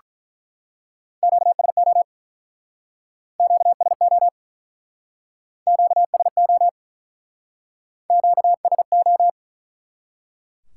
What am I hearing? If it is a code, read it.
Morse code, heard as a keyed tone, about 48 wpm, 700 Hz. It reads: QSO QSO QSO Q S O